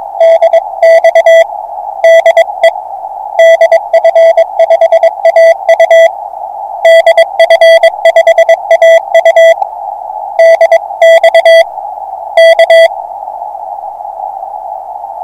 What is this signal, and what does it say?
Morse code, heard as a keyed tone, about 22 wpm, 660 Hz. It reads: DX DE DF5AU DF5AU DX K